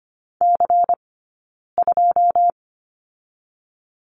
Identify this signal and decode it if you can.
Morse code, heard as a keyed tone, about 25 wpm, 700 Hz. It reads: C 2
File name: course / 25:10wpm / 092 Lesson 16d - Letter c - Two Characters 25wpm